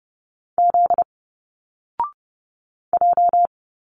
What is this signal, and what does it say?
Morse code, heard as a keyed tone, about 30 wpm, 700 Hz. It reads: Z J
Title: Single Letters 30wpm